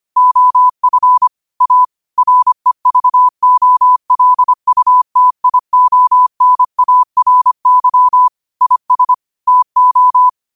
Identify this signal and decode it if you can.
Morse code, heard as a keyed tone, about 25 wpm, 1000 Hz. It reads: OF A REVOLUTIONARY IS TO